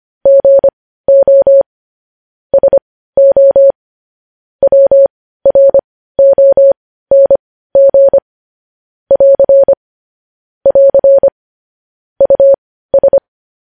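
Morse code, heard as a keyed tone, about 25 wpm, spaced as for 15 wpm, 550 Hz. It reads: GO SO WRONG <AR> <AR> US
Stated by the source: News headlines sent in Morse code at 15 words per minute, preceded by vvv.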